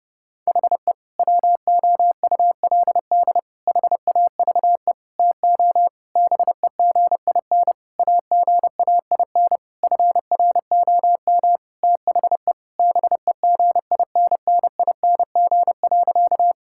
Morse code, heard as a keyed tone, about 32 wpm, 700 Hz. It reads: HE WOULD HAVE TO BEGIN AGAIN FROM THE BEGINNING.
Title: Sentences from Top 1000 Words - Review Part 5 - 30wpm